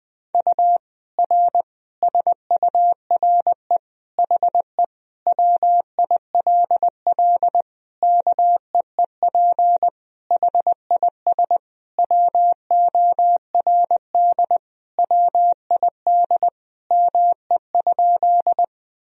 Morse code, heard as a keyed tone, about 21 wpm, 700 Hz. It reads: U R SURE HE WILL KEEP HIS WORD WID ME?
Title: Sentences from Top 300 Words and Abbreviations 20wpm